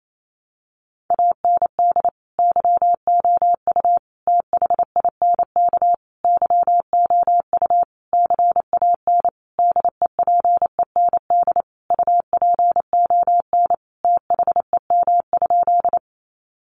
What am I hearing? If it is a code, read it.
Morse code, heard as a keyed tone, about 28 wpm, 700 Hz. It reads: AND YOU THINK YOU CAN DEPEND UPON THEM?
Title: Sentences from Top 1300 Words 28wpm